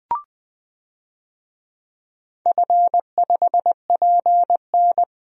Morse code, heard as a keyed tone, about 20 wpm, 700 Hz. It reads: F5PN
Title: DXCC-100 Call Signs - 4 Characters - Repeated 2x - 20wpm